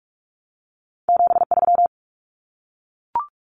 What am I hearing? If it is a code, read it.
Morse code, heard as a keyed tone, about 44 wpm, 700 Hz. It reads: 73
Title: Abbreviations 45wpm